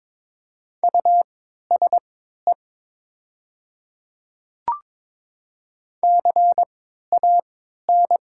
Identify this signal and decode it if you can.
Morse code, heard as a keyed tone, about 22 wpm, spaced as for 12 wpm, 700 Hz. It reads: USE CAN
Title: 3x Character Spacing - Top 200 Words - 22wpm